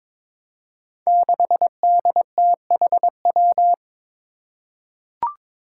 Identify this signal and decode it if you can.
Morse code, heard as a keyed tone, about 22 wpm, 700 Hz. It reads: 6DTHW
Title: Five Challenging Letter-Number Combinations 22wpm